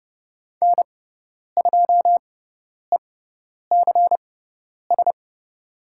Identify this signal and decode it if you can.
Morse code, heard as a keyed tone, about 30 wpm, spaced as for 10 wpm, 700 Hz. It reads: N2ECS
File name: Lesson 18g - Letter m - Call Signs 30wpm